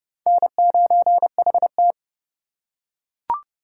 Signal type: Morse code

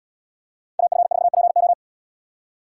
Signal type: Morse code

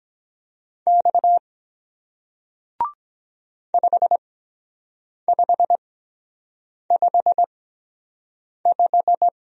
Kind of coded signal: Morse code

